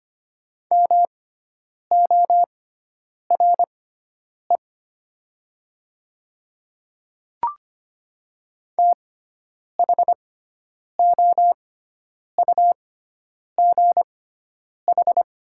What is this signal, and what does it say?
Morse code, heard as a keyed tone, about 25 wpm, spaced as for 9 wpm, 700 Hz. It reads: MORE THOUGH
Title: 6x Character Spacing - Top 500 Words - 25wpm